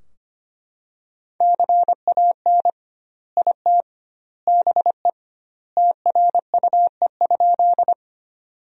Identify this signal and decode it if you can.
Morse code, heard as a keyed tone, about 25 wpm, 700 Hz. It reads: CAN IT BE TRUE?
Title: 2x Word-Spacing - Sentences from Top 400 Words - 25wpm